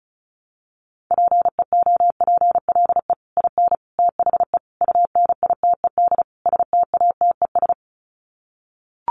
Morse code, heard as a keyed tone, about 34 wpm, 700 Hz. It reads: PEOPLE IN THE UNITED STATES
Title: Sets of 5 Words Encore 35wpm